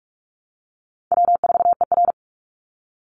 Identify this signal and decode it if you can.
Morse code, heard as a keyed tone, about 44 wpm, 700 Hz. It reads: W4ER
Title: Easy Call Signs Repeated 3x - 45wpm